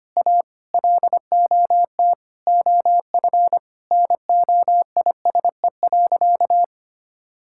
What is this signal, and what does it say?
Morse code, heard as a keyed tone, about 25 wpm, 700 Hz. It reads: A LOT OF NOISE.
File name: practice / 25wpm / Sentences with 2-Word Phrases - 25wpm